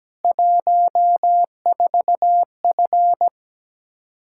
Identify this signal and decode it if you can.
Morse code, heard as a keyed tone, about 17 wpm, 700 Hz. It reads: 14F
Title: K1USN SST QSO Practice 17wpm